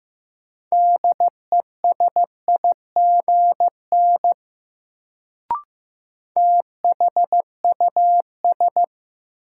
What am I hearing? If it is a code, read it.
Morse code, heard as a keyed tone, about 15 wpm, 700 Hz. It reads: DESIGN THUS
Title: Sentences from Top 800 Words - New Word Drill - 15wpm